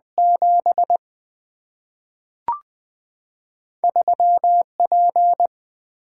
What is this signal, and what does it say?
Morse code, heard as a keyed tone, about 20 wpm, 700 Hz. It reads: 7 3P
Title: Single-Letter Single-Number 20wpm